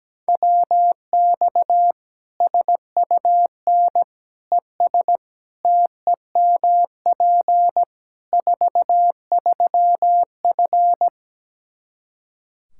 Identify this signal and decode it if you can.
Morse code, heard as a keyed tone, about 17 wpm, 700 Hz. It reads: WX SUN ES TEMP 43F
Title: Practice QSO - Exchange 2 - Info - 17wpm